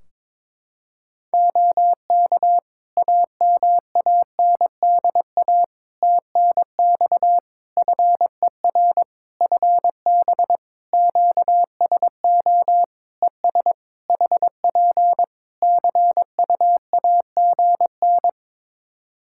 Morse code, heard as a keyed tone, about 22 wpm, 700 Hz. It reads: OK AMANDA TNX FER FB QSO ES HP CUAGN
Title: K1USN SST QSO Practice 22wpm